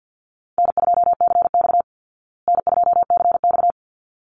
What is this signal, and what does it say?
Morse code, heard as a keyed tone, about 50 wpm, 700 Hz. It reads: N2CX N2CX
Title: Easy Call Signs Repeated 2x - 50wpm